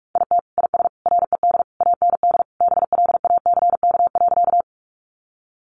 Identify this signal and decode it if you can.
Morse code, heard as a keyed tone, about 44 wpm, 700 Hz. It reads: IT IS RED AND BLACK.
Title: Easy Phrases 45wpm